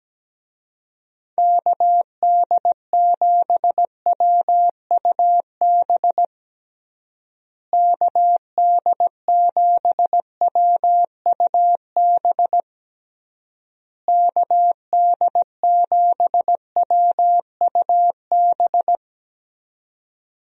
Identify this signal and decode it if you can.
Morse code, heard as a keyed tone, about 17 wpm, 700 Hz. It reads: KD7WUB KD7WUB KD7WUB
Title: USA Call Signs - 6 Characters - Repeated 3x - 17wpm